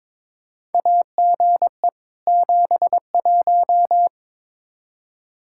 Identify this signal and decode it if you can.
Morse code, heard as a keyed tone, about 22 wpm, 700 Hz. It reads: AGE 71
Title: Practice QSO - Exchange 3 - More Info - 22wpm